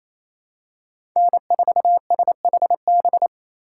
Morse code, heard as a keyed tone, about 28 wpm, 700 Hz. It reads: N4SHB
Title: USA Call Signs - 5 Characters - 2x Character Spacing - 28wpm